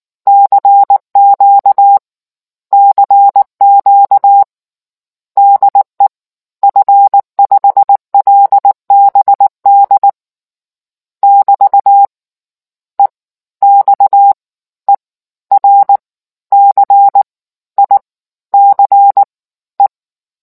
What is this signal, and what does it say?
Morse code, heard as a keyed tone, about 19 wpm, spaced as for 14 wpm, 800 Hz. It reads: CQ CQ DE F5LBD <BT> EXERCICE